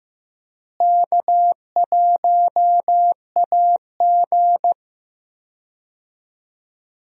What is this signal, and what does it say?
Morse code, heard as a keyed tone, about 15 wpm, 700 Hz. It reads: K1AG
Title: USA Call Signs - 4 Characters - Repeated 3x - 15wpm